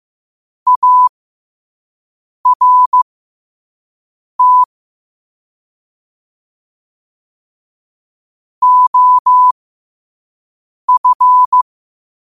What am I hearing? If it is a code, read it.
Morse code, heard as a keyed tone, about 15 wpm, spaced as for 5 wpm, 1000 Hz. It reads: ART OF
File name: qotdfiles / 5wpm